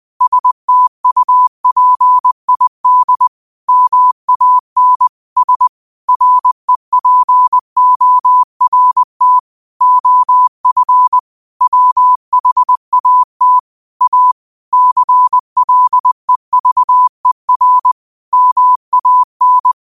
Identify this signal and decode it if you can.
Morse code, heard as a keyed tone, about 20 wpm, 1000 Hz. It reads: STUPID MAN S REPORT OF WHAT A CLEVER MAN